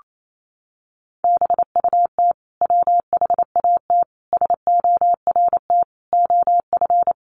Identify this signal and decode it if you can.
Morse code, heard as a keyed tone, about 28 wpm, 700 Hz. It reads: BUT WHAT SORT OF
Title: Sentences from Top 1300 Words 28wpm